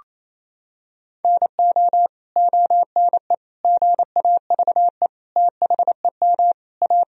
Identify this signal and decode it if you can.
Morse code, heard as a keyed tone, about 28 wpm, 700 Hz. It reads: NO ONE GAVE THEM A
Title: Sentences from Top 1000 Words - Review Part 1 - 28wpm